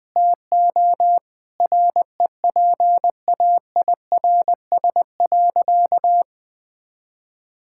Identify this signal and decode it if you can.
Morse code, heard as a keyed tone, about 20 wpm, 700 Hz. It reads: TO REPAIRS.
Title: Difficult Sentences with Hints - 20wpm